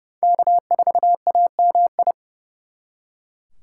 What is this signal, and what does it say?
Morse code, heard as a keyed tone, about 36 wpm, 700 Hz. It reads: K4AMI
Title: Speed Racing - USA 1x3 Call Signs 30wpm